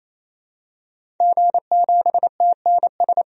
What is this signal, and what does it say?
Morse code, heard as a keyed tone, about 28 wpm, 700 Hz. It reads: G7TNS
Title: DXCC-100 Call Signs - 5 Characters - 28wpm